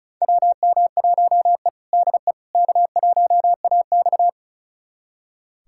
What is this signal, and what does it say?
Morse code, heard as a keyed tone, about 36 wpm, 700 Hz. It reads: WM1E DE K1AX
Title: USA Callsign DE Callsign - 4 Characters - 35wpm